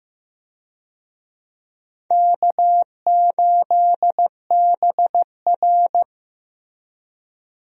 Morse code, heard as a keyed tone, about 15 wpm, 700 Hz. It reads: K8BR
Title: USA Call Signs - 4 Characters - Repeated 3x - 15wpm